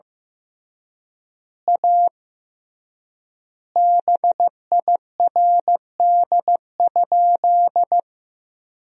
Morse code, heard as a keyed tone, about 15 wpm, 700 Hz. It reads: A BIRD?
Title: 3x Word-Spacing - Sentences from Top 400 Words - 15wpm